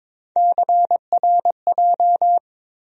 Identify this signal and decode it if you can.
Morse code, heard as a keyed tone, about 22 wpm, 700 Hz. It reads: CRJ
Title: USA Call Sign Postfixes - 3 Letters 22wpm